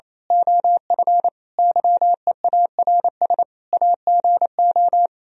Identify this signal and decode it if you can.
Morse code, heard as a keyed tone, about 27 wpm, 700 Hz. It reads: OF YEARS AGO
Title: Speed-Racing - Sets of 5 Words 28wpm